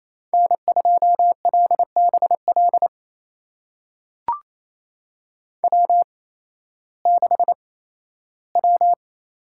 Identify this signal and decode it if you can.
Morse code, heard as a keyed tone, about 28 wpm, spaced as for 15 wpm, 700 Hz. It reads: N2LBL W6W
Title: USA Call Signs - 8x Character Spacing - 28wpm